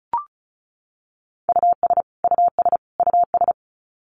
Morse code, heard as a keyed tone, about 34 wpm, 700 Hz. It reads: US US US
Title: Top 500 Words Repeated 35wpm